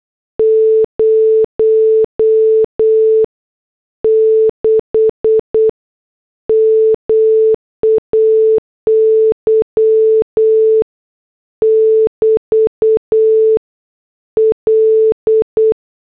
Morse code, heard as a keyed tone, about 8 wpm, 440 Hz. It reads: 06 MAY <BT> L